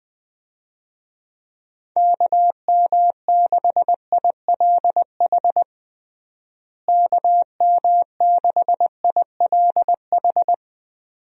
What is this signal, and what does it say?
Morse code, heard as a keyed tone, about 20 wpm, 700 Hz. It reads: KM6ILH KM6ILH
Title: USA Call Signs - 6 Characters - Repeated 2x - 20wpm